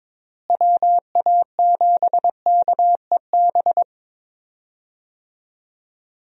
Morse code, heard as a keyed tone, about 22 wpm, 700 Hz. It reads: WA7KEB
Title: USA Call Signs - 6 Characters - Repeated 2x - 22wpm